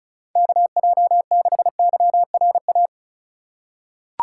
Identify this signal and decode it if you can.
Morse code, heard as a keyed tone, about 34 wpm, 700 Hz. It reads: KJ6YRA
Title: USA Call Signs - 6 Characters - 4x Character Spacing - 35wpm